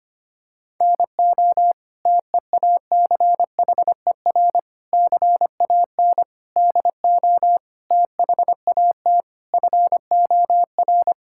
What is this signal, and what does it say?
Morse code, heard as a keyed tone, about 25 wpm, 700 Hz. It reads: NO TEACHER CAN DO THAT FOR
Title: Sentences from Top 1000 Words - Review Part 6 - 25wpm